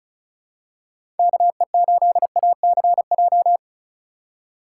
Morse code, heard as a keyed tone, about 36 wpm, 700 Hz. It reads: KE8ACJ
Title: USA Call Signs - 6 Characters - 3x Character Spacing - 35wpm